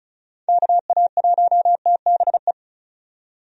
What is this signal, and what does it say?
Morse code, heard as a keyed tone, about 34 wpm, 700 Hz. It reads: KA1TBE
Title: USA Call Signs - 6 Characters - 35wpm